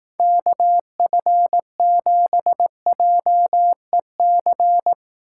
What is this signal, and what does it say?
Morse code, heard as a keyed tone, about 18 wpm, 700 Hz. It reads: KF7JEC
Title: Speed Racing - USA 2x3 Call Signs 15wpm